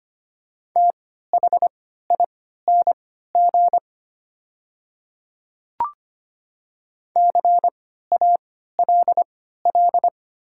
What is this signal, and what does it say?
Morse code, heard as a keyed tone, about 25 wpm, spaced as for 14 wpm, 700 Hz. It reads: THING CALL
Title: 3x Character Spacing - Top 100 Words - 25wpm